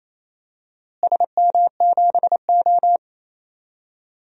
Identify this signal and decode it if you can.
Morse code, heard as a keyed tone, about 28 wpm, 700 Hz. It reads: SM7O
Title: DXCC-100 Call Signs - 4 Characters - Repeated 3x - 28wpm